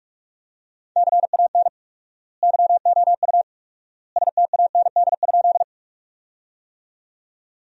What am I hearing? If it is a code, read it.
Morse code, heard as a keyed tone, about 47 wpm, 700 Hz. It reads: CAN YOU STAND?
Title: 4x Word-Spacing - Sentences from Top 300 Words - 45wpm